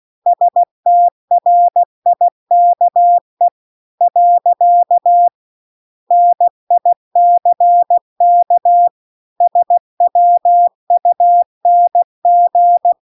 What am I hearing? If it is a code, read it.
Morse code, heard as a keyed tone, about 16 wpm, 700 Hz. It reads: STRIKE . NICK SWUNG